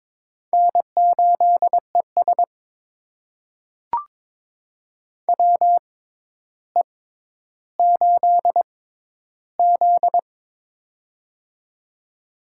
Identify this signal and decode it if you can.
Morse code, heard as a keyed tone, about 21 wpm, spaced as for 11 wpm, 700 Hz. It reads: N8ES WE8Z
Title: USA Call Signs - 4 Characters - 6x Character Spacing - 22wpm